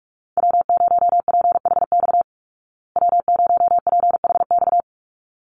Morse code, heard as a keyed tone, about 44 wpm, 700 Hz. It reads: W0PHX W0PHX
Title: Call Signs Repeated 3x - 45wpm